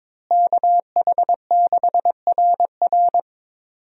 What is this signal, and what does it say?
Morse code, heard as a keyed tone, about 22 wpm, 700 Hz. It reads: KH6RR
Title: USA Call Signs - 5 Characters - Repeated 2x - 22wpm